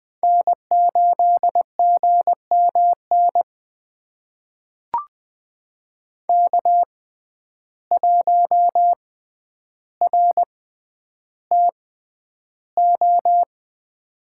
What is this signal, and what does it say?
Morse code, heard as a keyed tone, about 20 wpm, spaced as for 10 wpm, 700 Hz. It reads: N8GMN K1RTO
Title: USA Call Signs - 6x Character Spacing - 20wpm